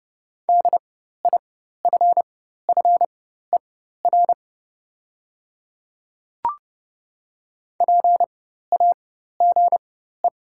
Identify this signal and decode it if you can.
Morse code, heard as a keyed tone, about 30 wpm, spaced as for 14 wpm, 700 Hz. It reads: DIFFER PAGE
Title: 4x Character Spacing - Top 300 Words - 30wpm